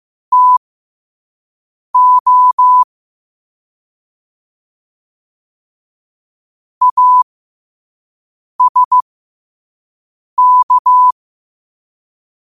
Morse code, heard as a keyed tone, about 15 wpm, spaced as for 5 wpm, 1000 Hz. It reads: TO ASK
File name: qotdfiles / 5wpm